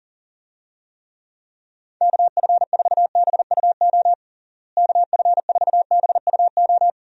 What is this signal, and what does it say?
Morse code, heard as a keyed tone, about 37 wpm, 700 Hz. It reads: KF4BUO KF4BUO
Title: USA Call Signs - 6 Characters - Repeated 3x - 40wpm